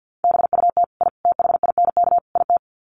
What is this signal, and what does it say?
Morse code, heard as a keyed tone, about 50 wpm, 700 Hz. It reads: BUT I THINK IT
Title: Sets of 4 Words 50wpm